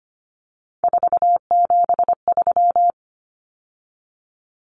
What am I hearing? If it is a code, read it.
Morse code, heard as a keyed tone, about 25 wpm, 700 Hz. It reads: 473
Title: Numbers - 3-Digit 25wpm